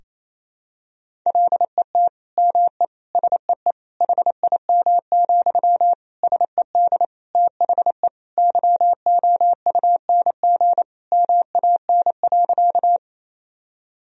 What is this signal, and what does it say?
Morse code, heard as a keyed tone, about 27 wpm, 700 Hz. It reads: LET ME SEE HIM, SED THE YOUNG MAN.
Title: Sentences from Top 400 Words and Abbreviations 28wpm